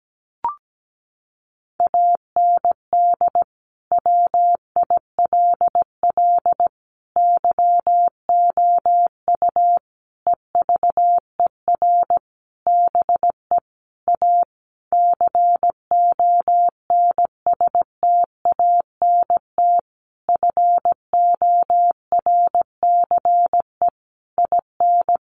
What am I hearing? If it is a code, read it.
Morse code, heard as a keyed tone, about 17 wpm, 700 Hz. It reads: AND WILL YOU EVER BE A CONSTANT FORCE IN
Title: Sentences from Top 1500 Words 17wpm